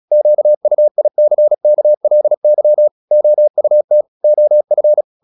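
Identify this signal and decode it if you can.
Morse code, heard as a keyed tone, about 37 wpm, 600 Hz. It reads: QUICKLY OUT OF
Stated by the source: Morse practice at 36 wpm, text of Ken Follett's novel Never, from CW med Gnister.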